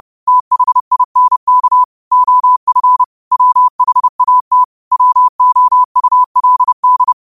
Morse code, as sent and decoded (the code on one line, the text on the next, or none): - .... .. -. -.- / --- ..-. / .-- .... .- - / .-- --- ..- .-.. -..
THINK OF WHAT WOULD